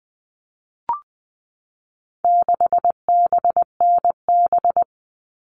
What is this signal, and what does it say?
Morse code, heard as a keyed tone, about 20 wpm, 700 Hz. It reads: 6BNB
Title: All Cut Numbers - 4-Digit 20wpm